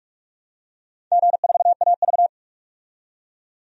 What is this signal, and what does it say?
Morse code, heard as a keyed tone, about 41 wpm, 700 Hz. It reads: G4AV
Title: DXCC-100 Call Signs - 4 Characters - 45wpm